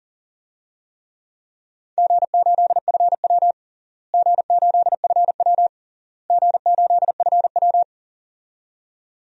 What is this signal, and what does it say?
Morse code, heard as a keyed tone, about 40 wpm, 700 Hz. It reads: G8FW G8FW G8FW
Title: DXCC-100 Call Signs - 4 Characters - Repeated 3x - 40wpm